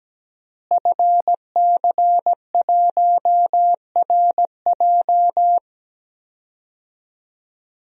Morse code, heard as a keyed tone, about 17 wpm, 700 Hz. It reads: FC1RJ
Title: DXCC-100 Call Signs - 5 Characters - Repeated 2x - 17wpm